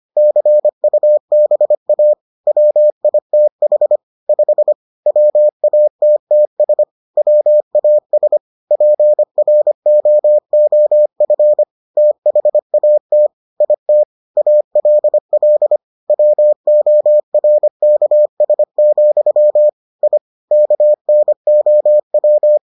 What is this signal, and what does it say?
Morse code, heard as a keyed tone, about 25 wpm, 600 Hz. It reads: CUBA WITH 5 WATTS WAS PROOF THAT IT ALL WORKS, I KNOW